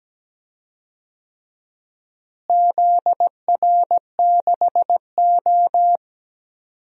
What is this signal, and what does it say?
Morse code, heard as a keyed tone, about 17 wpm, 700 Hz. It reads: ZR6O
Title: DXCC-100 Call Signs - 4 Characters - Repeated 2x - 17wpm